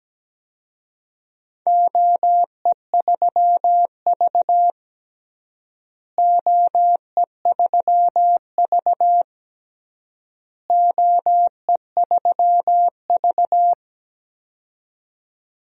Morse code, heard as a keyed tone, about 17 wpm, 700 Hz. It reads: OE3V OE3V OE3V
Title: DXCC-100 Call Signs - 4 Characters - Repeated 3x - 17wpm